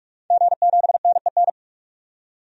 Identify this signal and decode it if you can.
Morse code, heard as a keyed tone, about 42 wpm, 700 Hz. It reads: G7NEN
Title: DXCC-100 Call Signs - 5 Characters - 45wpm